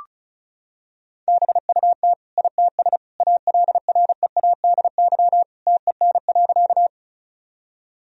Morse code, heard as a keyed tone, about 34 wpm, 700 Hz. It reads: BUT ITS ALREADY TEN.
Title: Sentences from Top 1000 Words - Review Part 2 - 35wpm